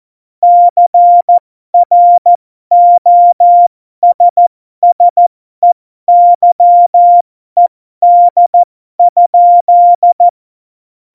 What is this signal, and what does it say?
Morse code, heard as a keyed tone, about 14 wpm, spaced as for 12 wpm, 700 Hz. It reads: CROSSEYED?
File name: Morsekurs / Witze